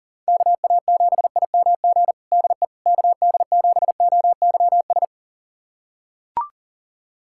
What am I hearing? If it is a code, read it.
Morse code, heard as a keyed tone, about 40 wpm, 700 Hz. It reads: KA7IMG DE KD7OYS
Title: USA Callsign DE Callsign - 6 Characters - 40wpm